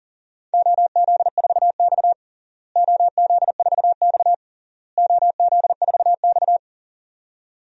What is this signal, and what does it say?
Morse code, heard as a keyed tone, about 37 wpm, 700 Hz. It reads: OZ4X OZ4X OZ4X
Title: DXCC-100 Call Signs - 4 Characters - Repeated 3x - 40wpm